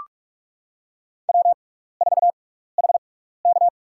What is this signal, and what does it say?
Morse code, heard as a keyed tone, about 42 wpm, spaced as for 15 wpm, 700 Hz. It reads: W4HK
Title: USA Call Signs - 4 Characters - 6x Character Spacing - 45wpm